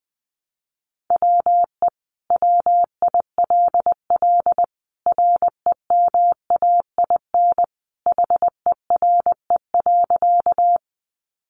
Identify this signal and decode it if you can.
Morse code, heard as a keyed tone, about 20 wpm, 700 Hz. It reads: WE WILL REMAIN HERE.
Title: Sentences from Top 1200 Words 20wpm